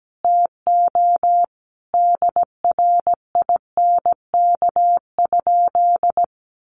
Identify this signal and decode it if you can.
Morse code, heard as a keyed tone, about 17 wpm, 700 Hz. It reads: TO DRINK?
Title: Sentences from Top 1500 Words 17wpm